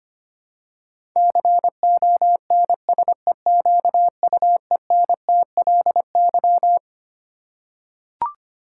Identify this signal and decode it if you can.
Morse code, heard as a keyed tone, about 25 wpm, 700 Hz. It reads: CONSEQUENTLY
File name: Twelve Letter Words - 25wpm